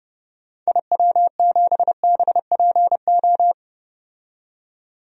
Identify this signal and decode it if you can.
Morse code, heard as a keyed tone, about 31 wpm, 700 Hz. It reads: IW7BPO
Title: DXCC-100 Call Signs - 6 Characters - 30wpm